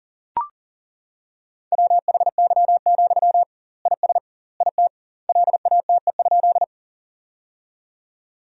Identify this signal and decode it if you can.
Morse code, heard as a keyed tone, about 40 wpm, 700 Hz. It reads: WHY, IS IT LATE?
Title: 2x Word-Spacing - Sentences from Top 300 Words - 40wpm